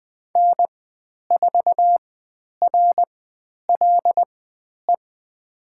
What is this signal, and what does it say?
Morse code, heard as a keyed tone, about 20 wpm, spaced as for 10 wpm, 700 Hz. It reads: N4RLE
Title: 109 Lesson 18g - Letter m - Call Signs 20wpm